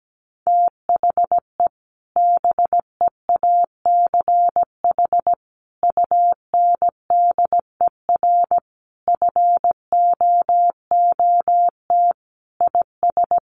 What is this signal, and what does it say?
Morse code, heard as a keyed tone, about 17 wpm, 700 Hz. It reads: THE BEACH UNDER FOOT IS
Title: Sentences from Top 1500 Words 17wpm